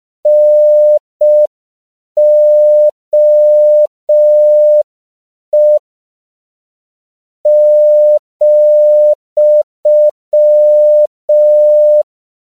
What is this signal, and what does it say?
Morse code, heard as a keyed tone, about 5 wpm, 600 Hz. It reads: NOE ,